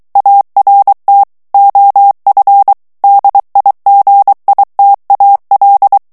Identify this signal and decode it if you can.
Morse code, heard as a keyed tone, about 23 wpm, 800 Hz. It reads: ART OF DIGITAL
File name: ANGLAIS texte clair 3 vitesse 22 mots minute